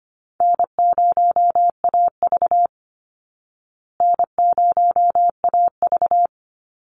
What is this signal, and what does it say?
Morse code, heard as a keyed tone, about 25 wpm, 700 Hz. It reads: N0AV N0AV
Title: Easy Call Signs Repeated 3x - 25wpm